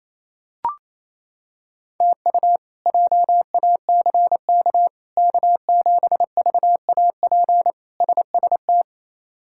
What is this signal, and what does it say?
Morse code, heard as a keyed tone, about 28 wpm, 700 Hz. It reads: TU JACK K7VAP SST
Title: K1USN SST QSO Practice - 28wpm